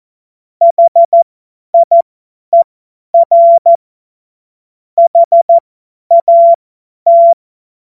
Morse code, heard as a keyed tone, about 14 wpm, spaced as for 10 wpm, 680 Hz. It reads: HIER HAT
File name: Witz-05-August-2025-50-F70